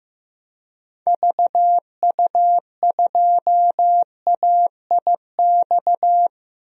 Morse code, heard as a keyed tone, about 15 wpm, 700 Hz. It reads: VU2AIX
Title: DXCC-100 Call Signs - 6 Characters - Repeated 2x - 15wpm